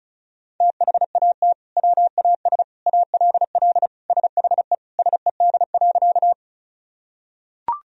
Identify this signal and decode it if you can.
Morse code, heard as a keyed tone, about 36 wpm, 700 Hz. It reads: THAT WAS ALL SHE SED.
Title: Sentences from Top 100 Words and Abbreviations 35wpm